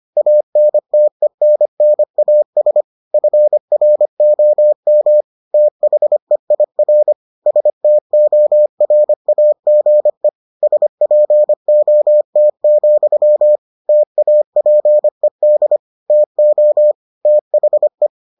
Morse code, heard as a keyed tone, about 25 wpm, 600 Hz. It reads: ANTENNAS FROM THEIR STORAGE SPOT, TAPED TO THE